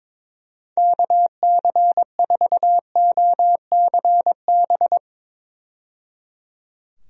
Morse code, heard as a keyed tone, about 22 wpm, 700 Hz. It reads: KC4OCB
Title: USA Call Signs - 6 Characters - 22wpm